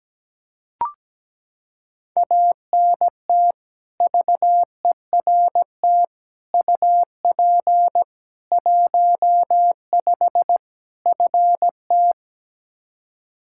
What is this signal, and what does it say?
Morse code, heard as a keyed tone, about 17 wpm, 700 Hz. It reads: ANT VERT UP 15 FT
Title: K1USN SST QSO Practice 17wpm